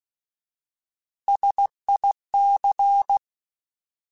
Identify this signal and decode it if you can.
Morse code, heard as a keyed tone, about 16 wpm, 800 Hz. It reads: SIC